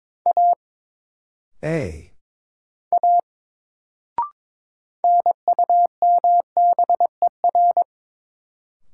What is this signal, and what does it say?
Morse code, heard as a keyed tone, about 22 wpm, 700 Hz. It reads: A A NUMBER
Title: Sentences from Top 100 Words - New Word Drill - 22wpm